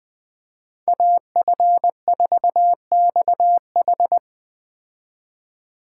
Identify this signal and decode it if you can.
Morse code, heard as a keyed tone, about 20 wpm, 700 Hz. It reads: AF4XH
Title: Speed Racing - USA 2x2 Call Signs 17wpm